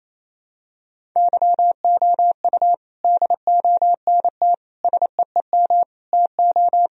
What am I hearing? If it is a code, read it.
Morse code, heard as a keyed tone, about 28 wpm, 700 Hz. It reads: YOU DONT SEEM TO
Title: Sentences from Top 1000 Words - Review Part 1 - 28wpm